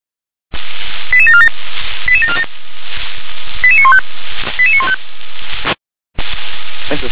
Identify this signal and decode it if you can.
Five-tone selective call (ZVEI), heard as quick runs of five tones, tones of about 70 ms, with speech. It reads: 88747 88747 88915 88915
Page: /357/SDR/ZVEI/